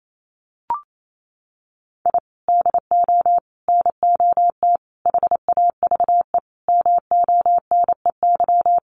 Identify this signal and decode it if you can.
Morse code, heard as a keyed tone, about 28 wpm, 700 Hz. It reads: I DO NOT HAVE MONEY
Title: Sentences from Top 1400 Words 28wpm